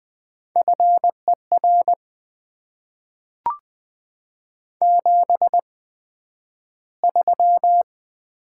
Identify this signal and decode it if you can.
Morse code, heard as a keyed tone, about 20 wpm, spaced as for 11 wpm, 700 Hz. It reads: FER 73
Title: Top 20 QSO Elements - 8x Character Spacing - 20wpm